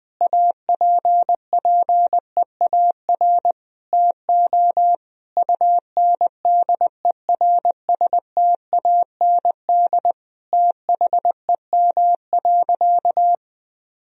Morse code, heard as a keyed tone, about 20 wpm, 700 Hz. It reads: APPEAR TO UNDERSTAND THEM.
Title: Sentences from Top 1000 Words - Review Part 2 - 20wpm